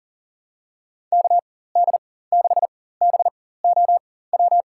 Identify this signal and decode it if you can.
Morse code, heard as a keyed tone, about 40 wpm, spaced as for 19 wpm, 700 Hz. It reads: KD6BOW
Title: USA Call Signs - 4x Character Spacing - 40wpm